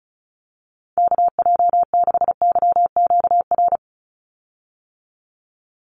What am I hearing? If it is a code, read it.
Morse code, heard as a keyed tone, about 34 wpm, 700 Hz. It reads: KJ6YQR
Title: USA Call Signs 35wpm